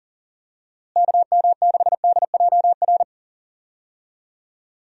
Morse code, heard as a keyed tone, about 40 wpm, 700 Hz. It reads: KM6DJR
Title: USA Call Signs - 6 Characters - Repeated 3x - 40wpm